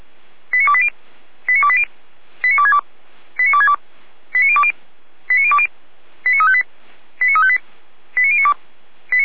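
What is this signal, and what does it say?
Five-tone selective call (ZVEI), heard as quick runs of five tones, tones of about 70 ms. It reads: 89280 89280 89262 89262 89020 89020 89358 89358 89092
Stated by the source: SDR-received ZVEI five-tone selective call, fire brigade Feuerwehr Orlamünde.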